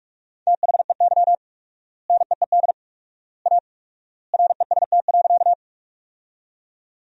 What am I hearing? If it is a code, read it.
Morse code, heard as a keyed tone, about 46 wpm, 700 Hz. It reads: THEY NEED A REST.
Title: 4x Word-Spacing - Sentences from Top 500 Words - 45wpm